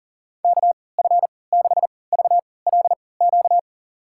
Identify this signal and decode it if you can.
Morse code, heard as a keyed tone, about 40 wpm, spaced as for 23 wpm, 700 Hz. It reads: KF6VLQ
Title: USA Call Signs - 6 Characters - 3x Character Spacing - 40wpm